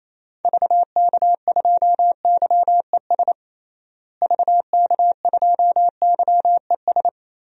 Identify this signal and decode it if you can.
Morse code, heard as a keyed tone, about 28 wpm, 700 Hz. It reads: VK2YES VK2YES
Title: DXCC-100 Call Signs - 6 Characters - Repeated 2x - 28wpm